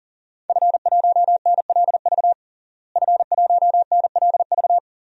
Morse code, heard as a keyed tone, about 40 wpm, 700 Hz. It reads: F1NLV F1NLV
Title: DXCC-100 Call Signs - 5 Characters - Repeated 3x - 40wpm